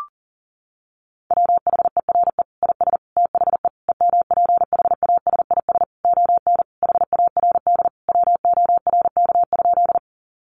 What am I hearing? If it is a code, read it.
Morse code, heard as a keyed tone, about 40 wpm, 700 Hz. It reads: WHERE IS THE EMPHASIS ON HARD WORK?
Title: Sentences from Top 1400 Words 40wpm